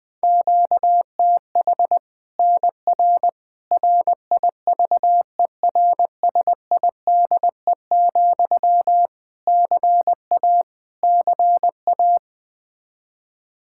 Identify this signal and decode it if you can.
Morse code, heard as a keyed tone, about 21 wpm, 700 Hz. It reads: QTH NR RIVERSIDE, CA CA
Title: Practice QSO - Exchange 1 - Beginning - 20wpm